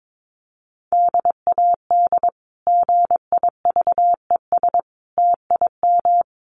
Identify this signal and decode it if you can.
Morse code, heard as a keyed tone, about 22 wpm, 700 Hz. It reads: DAD GIVES TIM
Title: Easy Phrases 22wpm